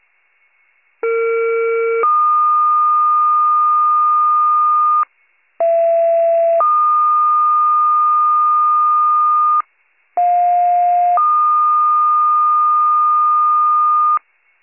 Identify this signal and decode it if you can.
Two-tone page, heard as a one-second tone, then a three-second tone, about 460 then 1220 Hz, about 670 then 1220 Hz, about 710 then 1220 Hz. It is heard from 1.0 to 14.2 s.